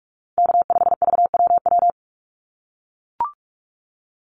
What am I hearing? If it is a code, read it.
Morse code, heard as a keyed tone, about 44 wpm, 700 Hz. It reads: K5VWW
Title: Call Signs - 45wpm